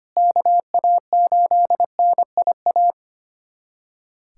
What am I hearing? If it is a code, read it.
Morse code, heard as a keyed tone, about 25 wpm, 700 Hz. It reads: KA8NIA
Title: USA Callsign DE Callsign - 6 Characters - 25wpm